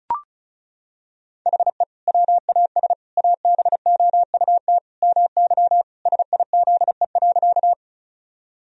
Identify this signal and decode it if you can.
Morse code, heard as a keyed tone, about 36 wpm, 700 Hz. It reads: HE WAS ABOUT MY SIZE.
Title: Sentences from Top 1000 Words - Review Part 2 - 35wpm